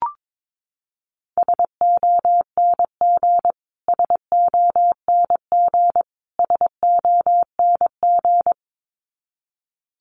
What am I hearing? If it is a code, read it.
Morse code, heard as a keyed tone, about 22 wpm, 700 Hz. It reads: SONG SONG SONG
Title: Top 500 Words Repeated 22wpm